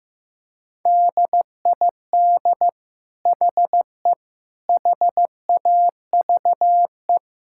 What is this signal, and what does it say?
Morse code, heard as a keyed tone, about 15 wpm, 700 Hz. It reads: DID HE HAVE
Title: Sentences from Top 1000 Words - Review Part 6 - 15wpm